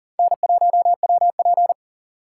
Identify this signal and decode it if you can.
Morse code, heard as a keyed tone, about 52 wpm, 700 Hz. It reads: N1WP